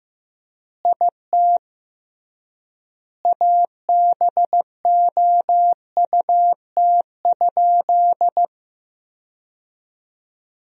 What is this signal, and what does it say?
Morse code, heard as a keyed tone, about 15 wpm, 700 Hz. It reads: IT ABOUT?
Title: 3x Word-Spacing - Sentences from Top 100 Words - 15wpm